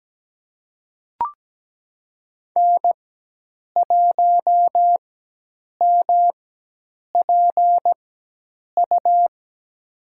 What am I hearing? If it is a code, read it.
Morse code, heard as a keyed tone, about 17 wpm, spaced as for 8 wpm, 700 Hz. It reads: N1MPU
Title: USA Call Signs - 5 Characters - 4x Character Spacing - 17wpm